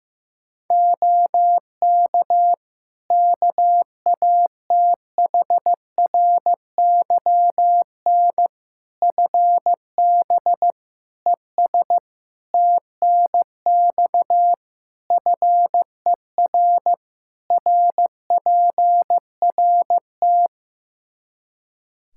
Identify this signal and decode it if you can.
Morse code, heard as a keyed tone, about 15 wpm, 700 Hz. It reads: OK KATHRYN FB ES TNX FER RPRT
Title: K1USN SST QSO Practice 15wpm